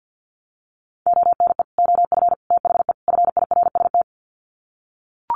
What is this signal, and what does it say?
Morse code, heard as a keyed tone, about 50 wpm, 700 Hz. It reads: ONE OF THE FIRST
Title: Sets of 4 Words 50wpm